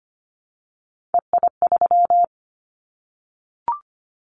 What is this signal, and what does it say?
Morse code, heard as a keyed tone, about 25 wpm, 700 Hz. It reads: EI3
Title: Callsign Prefixes for Top 50 DXCC - 25wpm